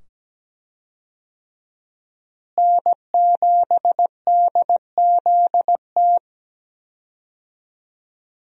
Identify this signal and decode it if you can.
Morse code, heard as a keyed tone, about 17 wpm, 700 Hz. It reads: N7DZT
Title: USA Call Signs - 5 Characters - Repeated 2x - 17wpm